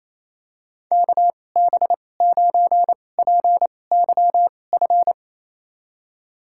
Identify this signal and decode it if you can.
Morse code, heard as a keyed tone, about 28 wpm, spaced as for 20 wpm, 700 Hz. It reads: KB9PYF
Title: USA Call Signs - 6 Characters - 2x Character Spacing - 28wpm